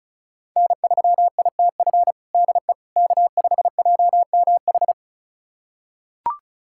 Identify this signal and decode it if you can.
Morse code, heard as a keyed tone, about 36 wpm, 700 Hz. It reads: N3ITF DE K5JMH
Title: USA Callsign DE Callsign - 5 Characters - 35wpm